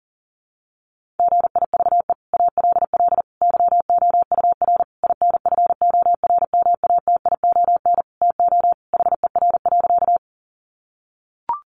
Morse code, heard as a keyed tone, about 37 wpm, 700 Hz. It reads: GIVE ALL YOUR INFORMATION TO HER.
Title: Sentences from Top 600 Words 40wpm